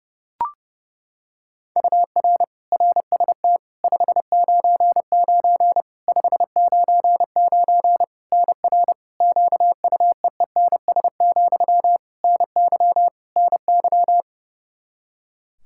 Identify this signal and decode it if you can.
Morse code, heard as a keyed tone, about 30 wpm, 700 Hz. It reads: UR RST 599 599 NR QUEENS, NY NY
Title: K1USN SST QSO Practice 30wpm